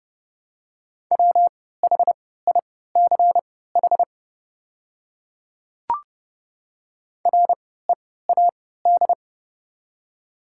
Morse code, heard as a keyed tone, about 30 wpm, spaced as for 17 wpm, 700 Hz. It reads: WHICH READ
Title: 3x Character Spacing - Top 300 Words - 30wpm